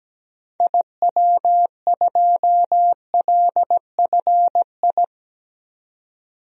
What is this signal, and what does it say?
Morse code, heard as a keyed tone, about 17 wpm, 700 Hz. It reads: IW2LFI
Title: DXCC-100 Call Signs - 6 Characters - 17wpm